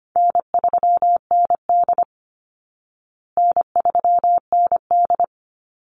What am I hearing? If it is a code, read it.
Morse code, heard as a keyed tone, about 25 wpm, 700 Hz. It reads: N3ND N3ND
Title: Easy Call Signs Repeated 3x - 25wpm